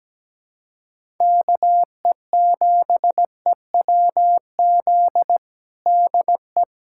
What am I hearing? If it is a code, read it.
Morse code, heard as a keyed tone, about 17 wpm, 700 Hz. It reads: KE7EWZ DE
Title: USA Callsign DE Callsign 17wpm